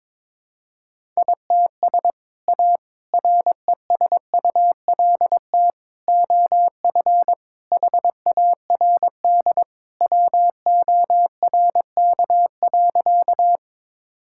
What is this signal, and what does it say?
Morse code, heard as a keyed tone, about 22 wpm, 700 Hz. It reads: ITS A RESULT OF HARD WORK.
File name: practice / 22wpm / Sentences with 3-Word Phrases - 22wpm